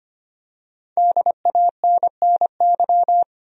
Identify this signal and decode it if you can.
Morse code, heard as a keyed tone, about 25 wpm, 700 Hz. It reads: DANNY
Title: USA Common Names - Max 5 Letters - 25wpm